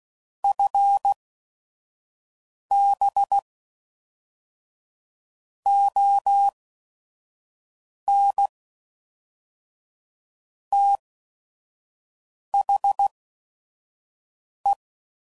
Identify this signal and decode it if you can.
Morse code, heard as a keyed tone, about 16 wpm, spaced as for 5 wpm, 800 Hz. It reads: FB ON THE
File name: CW Code Groups @5WPM / QSOs / WD0FBT 2